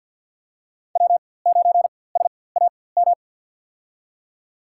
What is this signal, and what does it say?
Morse code, heard as a keyed tone, about 47 wpm, spaced as for 23 wpm, 700 Hz. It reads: W9SAM
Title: USA Call Signs - 5 Characters - 4x Character Spacing - 50wpm